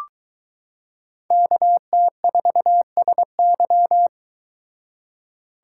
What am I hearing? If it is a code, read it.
Morse code, heard as a keyed tone, about 22 wpm, 700 Hz. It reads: KT4SY